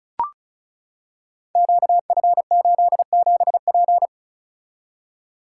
Morse code, heard as a keyed tone, about 36 wpm, 690 Hz. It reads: QF87P
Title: Code Groups - 5 Letter:Numbers - 35wpm